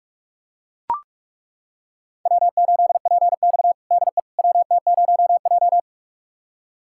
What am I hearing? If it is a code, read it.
Morse code, heard as a keyed tone, about 43 wpm, 700 Hz. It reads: W8PX DE WT0J
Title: USA Callsign DE Callsign - 4 Characters - 45wpm